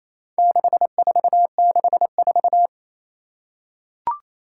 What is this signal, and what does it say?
Morse code, heard as a keyed tone, about 28 wpm, 700 Hz. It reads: 6464
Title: Four Challenging Letter-Number Combinations 28wpm